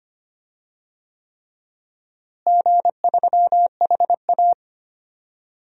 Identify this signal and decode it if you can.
Morse code, heard as a keyed tone, about 25 wpm, 700 Hz. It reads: G3HA
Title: DXCC-100 Call Signs - 4 Characters - Repeated 3x - 25wpm